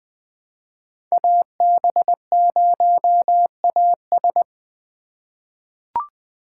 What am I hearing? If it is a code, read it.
Morse code, heard as a keyed tone, about 20 wpm, 700 Hz. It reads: AB0AS
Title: USA Call Signs - 5 Characters - 3x Character Spacing - 20wpm